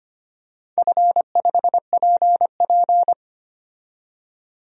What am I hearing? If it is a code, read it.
Morse code, heard as a keyed tone, about 25 wpm, 700 Hz. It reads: F5PP
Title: DXCC-100 Call Signs - 4 Characters - 25wpm